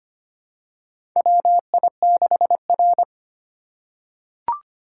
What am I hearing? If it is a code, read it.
Morse code, heard as a keyed tone, about 25 wpm, 700 Hz. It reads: WI6R
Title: USA Call Signs - 4 Characters - 2x Character Spacing - 25wpm